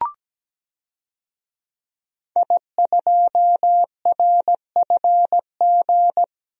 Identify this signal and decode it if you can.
Morse code, heard as a keyed tone, about 17 wpm, 700 Hz. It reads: I2RFG
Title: DXCC-100 Call Signs - 5 Characters - 17wpm